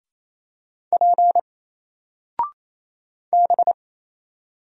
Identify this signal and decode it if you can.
Morse code, heard as a keyed tone, about 27 wpm, 700 Hz. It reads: P B
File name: Single-Letter-Number-28wpm